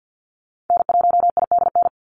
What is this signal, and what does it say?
Morse code, heard as a keyed tone, about 50 wpm, 700 Hz. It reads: N1IDN
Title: NA SOTA Activator Callsigns 50wpm